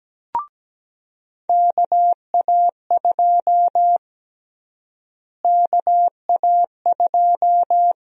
Morse code, heard as a keyed tone, about 17 wpm, 700 Hz. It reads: KA2 KA2
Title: USA Call Sign Prefixes - Repeated 2x - 17wpm